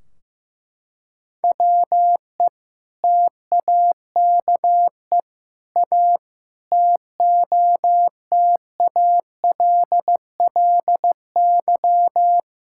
Morse code, heard as a keyed tone, about 15 wpm, 700 Hz. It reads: WE TAKE A TOTALLY
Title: Sentences from Top 1600 Words 15wpm